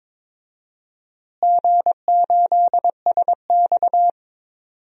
Morse code, heard as a keyed tone, about 23 wpm, 700 Hz. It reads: G8SX
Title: DXCC-100 Call Signs - 4 Characters - Repeated 2x - 22wpm